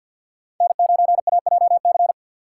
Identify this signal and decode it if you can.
Morse code, heard as a keyed tone, about 51 wpm, 700 Hz. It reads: N9AJC
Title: USA Call Signs - 4x Character Spacing - 50wpm